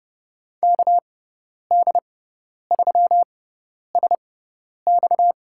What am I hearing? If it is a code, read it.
Morse code, heard as a keyed tone, about 32 wpm, spaced as for 10 wpm, 700 Hz. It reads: KD3SX
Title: USA Call Signs - 5 Characters - 6x Character Spacing - 30wpm